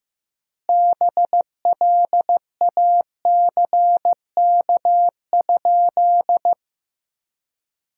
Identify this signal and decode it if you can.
Morse code, heard as a keyed tone, about 15 wpm, 700 Hz. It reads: BLACK?
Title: 4x Word-Spacing - Sentences from Top 400 Words - 15wpm